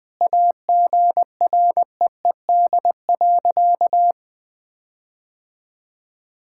Morse code, heard as a keyed tone, about 20 wpm, 700 Hz. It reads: AGREED.
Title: Sentences with 9-Word Phrases - Spoken Beginning - 20wpm